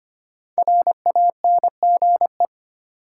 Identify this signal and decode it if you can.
Morse code, heard as a keyed tone, about 25 wpm, 700 Hz. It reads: RANGE
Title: Sets of 4 Words - New Word Drill - 25wpm